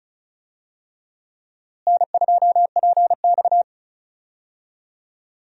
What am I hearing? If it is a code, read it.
Morse code, heard as a keyed tone, about 36 wpm, 700 Hz. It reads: N2PX
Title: USA Call Signs - 4 Characters - Repeated 3x - 35wpm